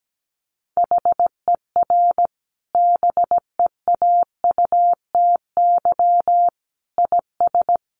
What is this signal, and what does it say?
Morse code, heard as a keyed tone, about 17 wpm, 700 Hz. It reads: HER BEAUTY IS
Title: Sentences from Top 1100 Words 17wpm